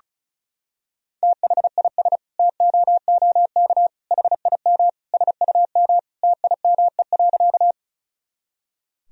Morse code, heard as a keyed tone, about 36 wpm, 700 Hz. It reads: THIS TOOK HIM SUM TIME.
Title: Sentences from Top 300 Words and Abbreviations 35wpm